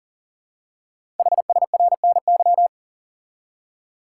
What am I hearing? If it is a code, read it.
Morse code, heard as a keyed tone, about 38 wpm, 690 Hz. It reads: HSRNY
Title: Code Groups - 5 Letters - 40wpm